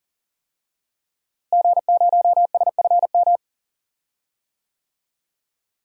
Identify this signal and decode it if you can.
Morse code, heard as a keyed tone, about 40 wpm, 700 Hz. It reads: G0SFM
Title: DXCC-100 Call Signs - 5 Characters - Repeated 2x - 40wpm